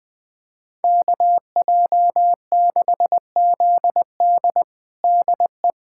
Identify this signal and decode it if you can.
Morse code, heard as a keyed tone, about 20 wpm, 700 Hz. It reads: KJ6ZD DE
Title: USA Callsign DE Callsign - 5 Characters - 20wpm